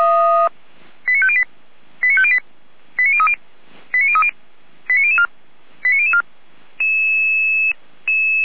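Five-tone selective call (ZVEI), heard as quick runs of five tones, tones of about 70 ms. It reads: 89508 89508 89030 89030 89004 89004
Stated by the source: ZVEI five-tone call received by SDR, recorded by Feuerwehr Orlamünde.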